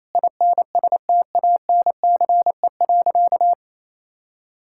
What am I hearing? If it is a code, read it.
Morse code, heard as a keyed tone, about 28 wpm, 700 Hz. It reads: INSTANCE.
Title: Sentences from Top 1000 Words - Review Part 5 - 28wpm